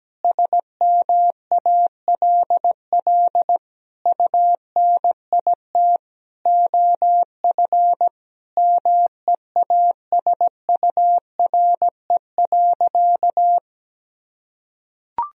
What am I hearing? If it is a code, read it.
Morse code, heard as a keyed tone, about 17 wpm, 700 Hz. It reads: SMALL UNIT OF MEASURE.
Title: Sentences from Top 1000 Words - Review Part 2 - 17wpm